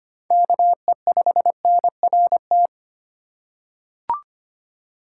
Morse code, heard as a keyed tone, about 25 wpm, 700 Hz. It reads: KE5NRT
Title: USA Call Signs - 6 Characters - 3x Character Spacing - 25wpm